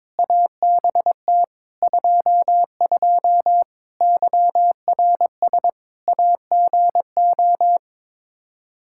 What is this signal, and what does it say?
Morse code, heard as a keyed tone, about 22 wpm, 700 Hz. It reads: ABT 22 YRS AGO
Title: Practice QSO - Exchange 3 - More Info - 22wpm